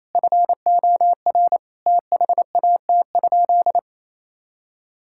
Morse code, heard as a keyed tone, about 28 wpm, 700 Hz. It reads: FOR THAT?
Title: Sentences from Top 1000 Words - Review Part 6 - 28wpm